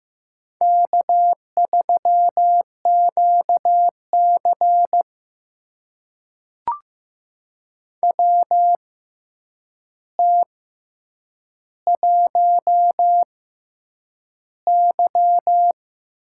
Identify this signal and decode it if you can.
Morse code, heard as a keyed tone, about 15 wpm, spaced as for 8 wpm, 700 Hz. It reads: K3QC WT1Y
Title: USA Call Signs - 4 Characters - 6x Character Spacing - 15wpm